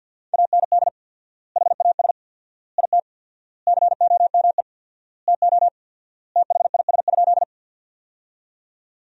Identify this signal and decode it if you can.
Morse code, heard as a keyed tone, about 50 wpm, 700 Hz. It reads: AND HAS IT COME TO THIS?
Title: 4x Word-Spacing - Sentences from Top 100 Words - 50wpm